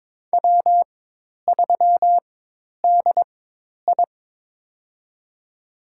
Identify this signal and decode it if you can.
Morse code, heard as a keyed tone, about 21 wpm, spaced as for 10 wpm, 700 Hz. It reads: W3DI
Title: USA Call Signs - 4 Characters - 4x Character Spacing - 22wpm